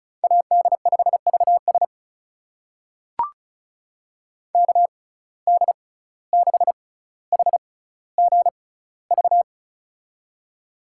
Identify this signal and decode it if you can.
Morse code, heard as a keyed tone, about 36 wpm, spaced as for 17 wpm, 700 Hz. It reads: AD5VS KD6HGV
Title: USA Call Signs - 6x Character Spacing - 35wpm